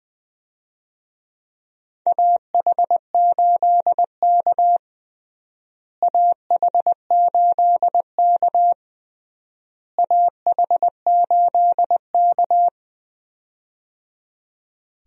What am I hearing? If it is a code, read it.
Morse code, heard as a keyed tone, about 20 wpm, 700 Hz. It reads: AH8K AH8K AH8K
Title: USA Call Signs - 4 Characters - Repeated 3x - 20wpm